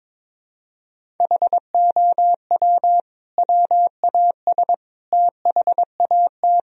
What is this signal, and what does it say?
Morse code, heard as a keyed tone, about 22 wpm, 700 Hz. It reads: HOW WAS THAT
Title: Sentences from Top 1000 Words - Review Part 4 - 22wpm